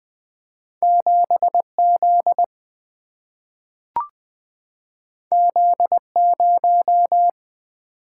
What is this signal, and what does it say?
Morse code, heard as a keyed tone, about 20 wpm, 700 Hz. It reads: 7Z Z0
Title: Single-Letter Single-Number 20wpm